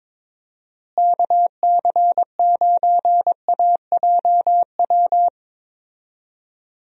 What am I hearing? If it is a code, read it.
Morse code, heard as a keyed tone, about 22 wpm, 700 Hz. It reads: KC9AJW
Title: USA Call Signs - 6 Characters - Repeated 3x - 22wpm